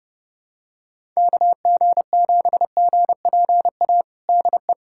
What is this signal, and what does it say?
Morse code, heard as a keyed tone, about 30 wpm, 700 Hz. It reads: KG7GPA DE
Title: USA Callsign DE Callsign 30wpm